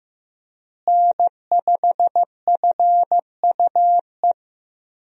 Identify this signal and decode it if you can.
Morse code, heard as a keyed tone, about 15 wpm, 700 Hz. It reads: N5FUE
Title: USA Call Signs - 5 Characters - Repeated 2x - 15wpm